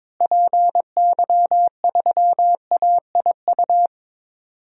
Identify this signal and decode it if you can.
Morse code, heard as a keyed tone, about 22 wpm, 700 Hz. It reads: PY3AIU
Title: DXCC-100 Call Signs - 6 Characters - Repeated 3x - 22wpm